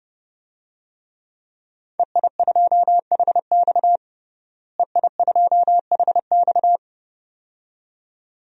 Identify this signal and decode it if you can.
Morse code, heard as a keyed tone, about 30 wpm, 700 Hz. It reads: EI2HX EI2HX
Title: DXCC-100 Call Signs - 5 Characters - Repeated 2x - 30wpm